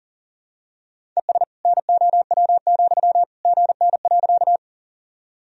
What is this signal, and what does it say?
Morse code, heard as a keyed tone, about 39 wpm, 700 Hz. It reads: ES NOW, GN.
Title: Sentences from Top 300 Words and Abbreviations 40wpm